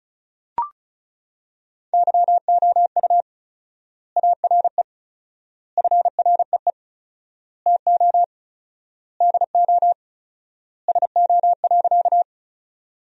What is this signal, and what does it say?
Morse code, heard as a keyed tone, about 34 wpm, 700 Hz. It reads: YOU ARE FREE TO DO SO.
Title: 4x Word-Spacing - Sentences from Top 500 Words - 35wpm